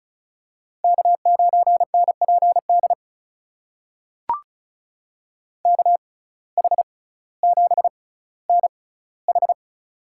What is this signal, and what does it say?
Morse code, heard as a keyed tone, about 36 wpm, spaced as for 18 wpm, 700 Hz. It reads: K9NPD KH7NH
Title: USA Call Signs - 5 Characters - 6x Character Spacing - 35wpm